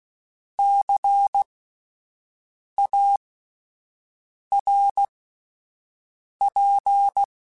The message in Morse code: CARP

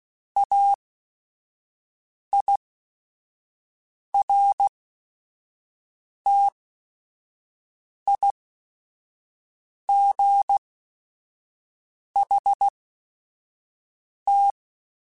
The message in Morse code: AIRTIGHT

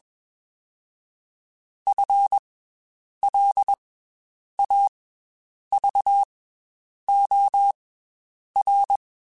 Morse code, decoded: F L A V O R

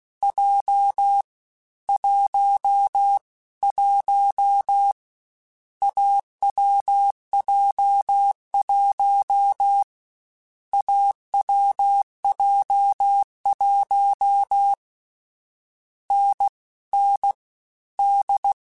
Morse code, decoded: J 1 1 AWJ1 AWJ1 N N D

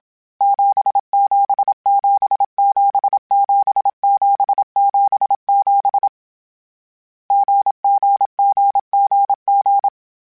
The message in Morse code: 77777777 GGGGG

